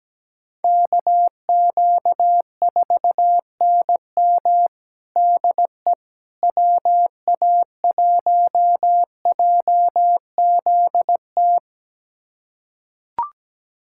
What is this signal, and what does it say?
Morse code, heard as a keyed tone, about 17 wpm, 700 Hz. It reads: KQ4NM DE WA1JZT
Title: USA Callsign DE Callsign 17wpm